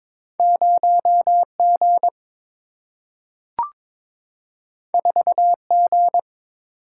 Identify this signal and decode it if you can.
Morse code, heard as a keyed tone, about 22 wpm, 700 Hz. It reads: 0 G 4 G
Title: Single-Letter Single-Number 22wpm